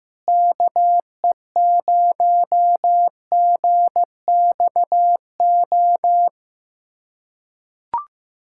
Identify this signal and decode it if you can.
Morse code, heard as a keyed tone, about 15 wpm, 700 Hz. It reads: KE0GXO
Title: USA Callsign DE Callsign - 6 Characters - 15wpm